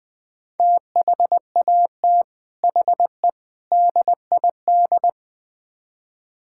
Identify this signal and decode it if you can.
Morse code, heard as a keyed tone, about 20 wpm, 700 Hz. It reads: THAT HE DID
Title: Build A Phrase - 4-Word Phrases - 20wpm